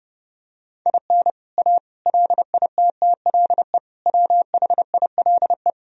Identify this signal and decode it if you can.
Morse code, heard as a keyed tone, about 30 wpm, 700 Hz. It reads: IN A LITTLE WHILE
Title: Sentences from Top 1000 Words - Review Part 1 - 30wpm